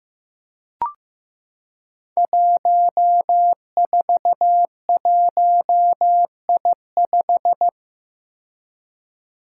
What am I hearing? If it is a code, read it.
Morse code, heard as a keyed tone, about 15 wpm, 700 Hz. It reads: 141I5
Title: Five Challenging Letter-Number Combinations 15wpm